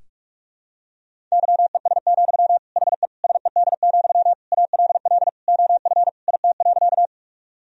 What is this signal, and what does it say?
Morse code, heard as a keyed tone, about 46 wpm, 700 Hz. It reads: YES, HE SED, ALL OF IT.
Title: Sentences from Top 500 Words and Abbreviations 45wpm